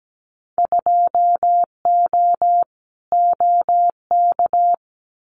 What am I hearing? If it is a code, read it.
Morse code, heard as a keyed tone, about 17 wpm, 700 Hz. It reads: 2O OK
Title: Winter Field Day Exchanges Encore 17wpm